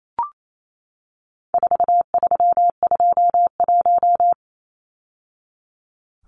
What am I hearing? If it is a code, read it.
Morse code, heard as a keyed tone, about 28 wpm, 700 Hz. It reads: 4321
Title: Numbers - 4-Digit 28wpm